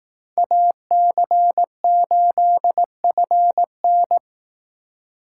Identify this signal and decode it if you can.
Morse code, heard as a keyed tone, about 18 wpm, 700 Hz. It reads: AC8FN
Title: Speed Racing - USA 2x2 Call Signs 15wpm